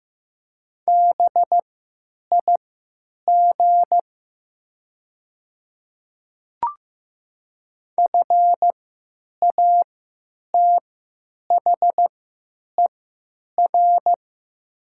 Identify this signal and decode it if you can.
Morse code, heard as a keyed tone, about 15 wpm, spaced as for 9 wpm, 700 Hz. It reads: BIG FATHER
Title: 3x Character Spacing - Top 200 Words - 15wpm